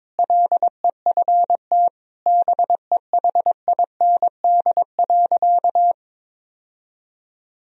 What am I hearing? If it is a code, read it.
Morse code, heard as a keyed tone, about 22 wpm, 700 Hz. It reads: LEFT BEHIND.
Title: Sentences from Top 1000 Words - Review Part 1 - 22wpm